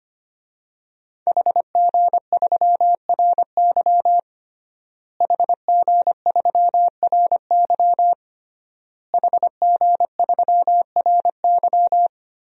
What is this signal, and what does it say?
Morse code, heard as a keyed tone, about 25 wpm, 700 Hz. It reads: HG3RY HG3RY HG3RY
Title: DXCC-100 Call Signs - 5 Characters - Repeated 3x - 25wpm